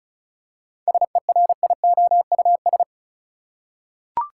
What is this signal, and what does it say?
Morse code, heard as a keyed tone, about 34 wpm, 700 Hz. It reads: SERIOUS
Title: Seven Letter Words - 35wpm